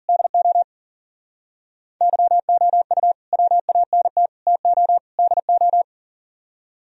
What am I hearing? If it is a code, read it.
Morse code, heard as a keyed tone, about 53 wpm, 700 Hz. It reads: DO YOU WANT TO DO